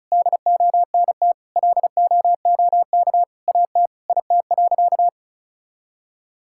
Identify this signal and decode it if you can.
Morse code, heard as a keyed tone, about 34 wpm, 700 Hz. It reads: DONT LOOK AT IT.
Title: Sentences from Top 1000 Words - Review Part 5 - 35wpm